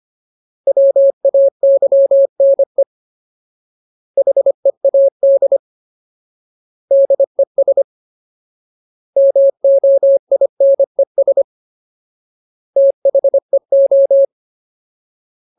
Morse code, heard as a keyed tone, about 25 wpm, 550 Hz. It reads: WAYNE HEAD DES MOINES THEO
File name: Ultimate Word List / 25WPM Ultimate